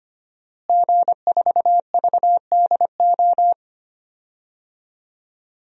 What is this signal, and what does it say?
Morse code, heard as a keyed tone, about 25 wpm, 700 Hz. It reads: G4VDO
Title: DXCC-100 Call Signs - 5 Characters - Repeated 3x - 25wpm